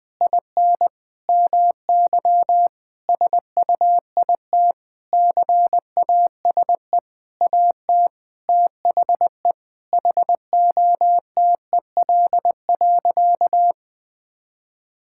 Morse code, heard as a keyed tone, about 21 wpm, 700 Hz. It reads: IN MY SUIT CASE AT THE HOTEL.
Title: Sentences from Top 1600 Words 20wpm